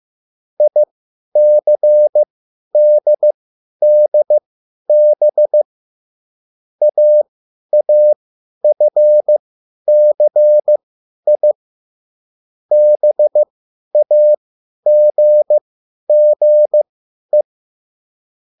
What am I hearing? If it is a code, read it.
Morse code, heard as a keyed tone, about 15 wpm, spaced as for 10 wpm, 600 Hz. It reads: ICDDB AAFCI BAGGE
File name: QSL / CW lesson